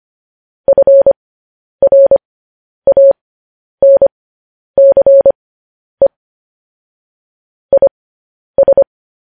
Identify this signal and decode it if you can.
Morse code, heard as a keyed tone, about 25 wpm, spaced as for 10 wpm, 550 Hz. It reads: FRANCE IS